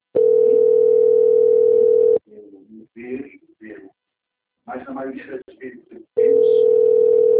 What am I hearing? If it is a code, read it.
Ringback tone (North America), 440 + 480 Hz, with speech.
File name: out- - - - - .20130